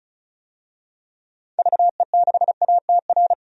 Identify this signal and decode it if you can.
Morse code, heard as a keyed tone, about 36 wpm, 700 Hz. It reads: VE6ATR
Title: DXCC-100 Call Signs - 6 Characters - 35wpm